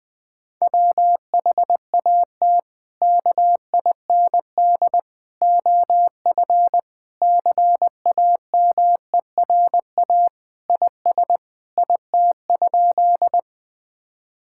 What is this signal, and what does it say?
Morse code, heard as a keyed tone, about 20 wpm, 700 Hz. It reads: WHAT KIND OF CAMERA IS IT?
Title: Sentences from Top 1000 Words - Review Part 5 - 20wpm